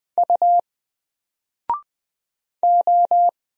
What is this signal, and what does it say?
Morse code, heard as a keyed tone, about 20 wpm, 700 Hz. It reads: U O
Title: Single Letter-Number 20wpm